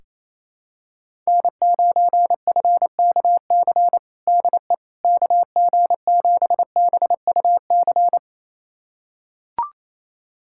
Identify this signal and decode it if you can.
Morse code, heard as a keyed tone, about 28 wpm, 700 Hz. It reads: N9FKC DE KG7BUC
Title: USA Callsign DE Callsign 28wpm